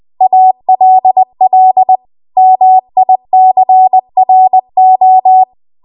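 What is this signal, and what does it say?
Morse code, heard as a keyed tone, about 20 wpm, 750 Hz. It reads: ALL MICRO
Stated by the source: Morse code practice at 20 WPM.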